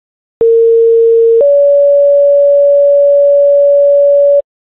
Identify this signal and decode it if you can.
Two-tone page, heard as a one-second tone, then a three-second tone, about 460 then 570 Hz. It is heard from 0.4 to 4.4 s.